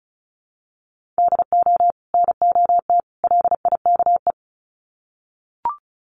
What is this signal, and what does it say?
Morse code, heard as a keyed tone, about 34 wpm, 700 Hz. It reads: DO NOT LIKE
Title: Sets of 3 Words 35wpm